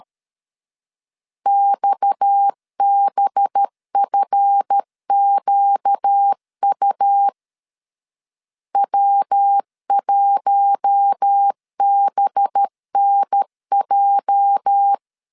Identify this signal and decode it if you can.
Morse code, heard as a keyed tone, about 13 wpm, 800 Hz. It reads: XBFQU W1BNJ
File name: LEZIONI / 8 YJ45 / 8 YJ 12wpm 2235